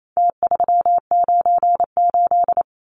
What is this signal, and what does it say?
Morse code, heard as a keyed tone, about 28 wpm, 700 Hz. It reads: T398
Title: Numbers - Contest Style - 4-Digit 28wpm